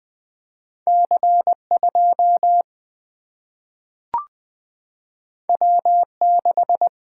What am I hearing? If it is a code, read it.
Morse code, heard as a keyed tone, about 20 wpm, 700 Hz. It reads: C2 W6
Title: Single-Letter Single-Number 20wpm